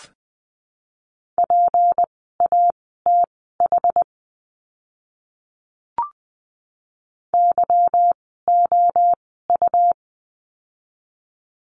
Morse code, heard as a keyed tone, about 20 wpm, spaced as for 15 wpm, 700 Hz. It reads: PATH YOU
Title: 153 Lesson 25f - Letter p - Words 20wpm